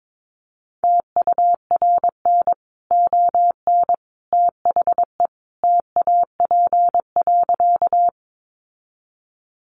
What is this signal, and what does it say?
Morse code, heard as a keyed tone, about 22 wpm, 700 Hz. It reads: TURN ON THE TAP.
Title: Sentences from Top 1400 Words 22wpm